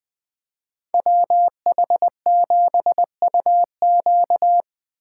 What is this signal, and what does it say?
Morse code, heard as a keyed tone, about 20 wpm, 700 Hz. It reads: WH7UQ
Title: USA Call Signs - 5 Characters - 3x Character Spacing - 20wpm